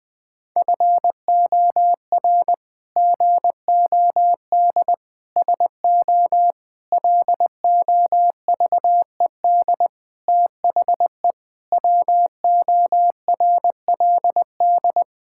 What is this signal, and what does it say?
Morse code, heard as a keyed tone, about 20 wpm, 700 Hz. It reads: FOR GOD SO LOVED THE WORLD
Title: Sentences with 8-Word Phrases - 20wpm